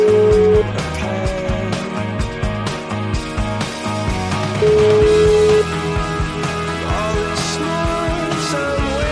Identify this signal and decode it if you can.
Ringback tone (Europe, 425 Hz), with noise.